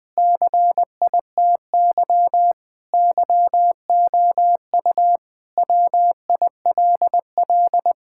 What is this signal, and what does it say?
Morse code, heard as a keyed tone, about 20 wpm, 700 Hz. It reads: CITY YOU WILL
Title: Sentences with 4-Word Phrases - 20wpm